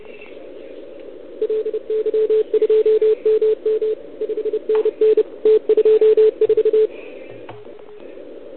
Morse code, heard as a keyed tone, about 30 wpm, 440 Hz. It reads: LY2MM 5NN T24
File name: ARCHIV / OKDX2020 / OL4Wa